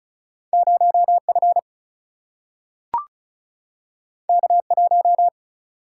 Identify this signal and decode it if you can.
Morse code, heard as a keyed tone, about 36 wpm, 700 Hz. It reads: 0F K1
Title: Single-Letter Single-Number 35wpm